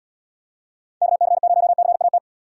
Morse code, heard as a keyed tone, about 74 wpm, 700 Hz. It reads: KC0CAT